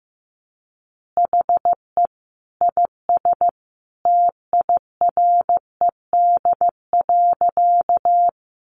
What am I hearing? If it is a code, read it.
Morse code, heard as a keyed tone, about 15 wpm, 700 Hz. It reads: HE IS TIRED.
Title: Additional Easy Phrases 15wpm